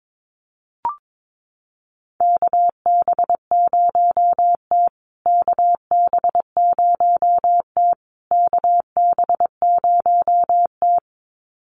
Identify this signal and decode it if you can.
Morse code, heard as a keyed tone, about 22 wpm, 700 Hz. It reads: KB0T KB0T KB0T
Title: USA Call Signs Repeated 3x 22wpm